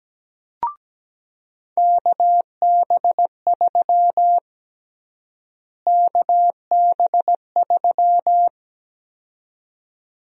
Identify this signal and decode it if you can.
Morse code, heard as a keyed tone, about 17 wpm, 700 Hz. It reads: KB3 KB3
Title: USA Call Sign Prefixes - Repeated 2x - 17wpm